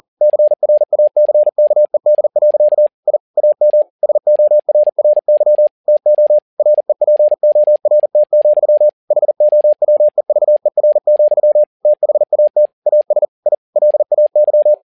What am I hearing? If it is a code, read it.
Morse code, heard as a keyed tone, about 40 wpm, 600 Hz. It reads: CRACKED. I AM SORRY TO REPORT, HOWEVER, THAT AS I LAY
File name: learning center / tower trouble / tower touble 40wpm